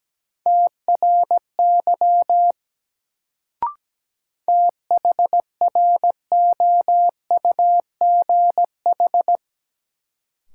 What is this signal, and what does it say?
Morse code, heard as a keyed tone, about 17 wpm, 700 Hz. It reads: TRY THROUGH
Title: Sets of 2 Words - New Word Drill - 17wpm